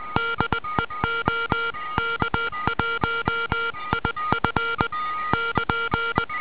Morse code, heard as a keyed tone, about 20 wpm, 1390 Hz, with noise. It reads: DE OK1IF <KN>
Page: IZ3DVW/beac 3.5770 JN55VF nr Monselice, PD 15m asl 11 s msg, 14 s carrier, ex 3582 kHz 15.09.2008 1 inv.v dipole IK3NWXP